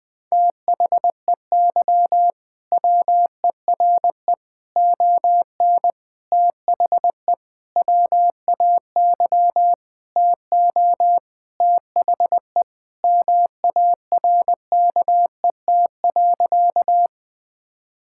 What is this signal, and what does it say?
Morse code, heard as a keyed tone, about 20 wpm, 700 Hz. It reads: THEY WERE ON THE WAY TO THE MARKET.
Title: Sentences with 5-Word Phrases - 20wpm